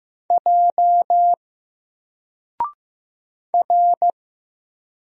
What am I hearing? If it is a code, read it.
Morse code, heard as a keyed tone, about 15 wpm, 700 Hz. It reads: J R
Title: Single Letter-Number 15wpm